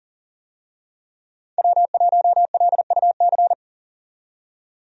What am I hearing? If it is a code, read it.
Morse code, heard as a keyed tone, about 40 wpm, 700 Hz. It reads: W1LUC
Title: USA Call Signs - 5 Characters - 40wpm